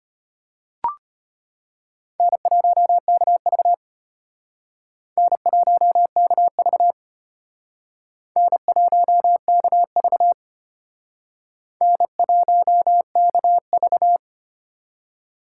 Morse code, heard as a keyed tone, about 32 wpm, 700 Hz. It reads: N1KV N1KV N1KV N1KV